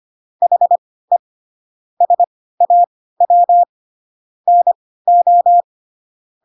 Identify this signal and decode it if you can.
Morse code, heard as a keyed tone, about 25 wpm, spaced as for 16 wpm, 700 Hz. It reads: HE SAW NO
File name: Pinocchio / Pinocchio ch1 10 25 ik1hge